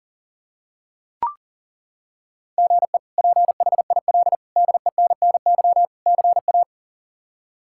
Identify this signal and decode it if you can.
Morse code, heard as a keyed tone, about 40 wpm, 690 Hz. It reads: GE PHIL DENNY CA
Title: K1USN SST QSO Practice - 40wpm